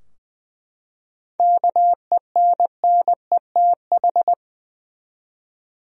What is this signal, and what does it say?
Morse code, heard as a keyed tone, about 20 wpm, 700 Hz. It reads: KENNETH
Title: USA Common Names - 20wpm